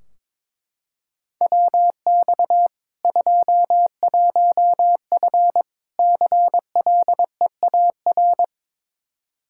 Morse code, heard as a keyed tone, about 22 wpm, 700 Hz. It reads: WX 21F CLEAR
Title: Practice QSO - Exchange 2 - Info - 22wpm